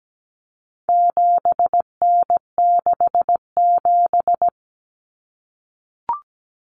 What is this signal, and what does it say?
Morse code, heard as a keyed tone, about 17 wpm, 700 Hz. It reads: 7N67
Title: Common Cut Numbers - 4-Digit 17wpm